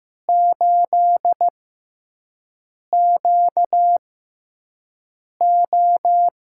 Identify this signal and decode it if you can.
Morse code, heard as a keyed tone, about 15 wpm, spaced as for 5 wpm, 700 Hz. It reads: 8QO
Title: USA Call Signs - 6x Character Spacing - 15wpm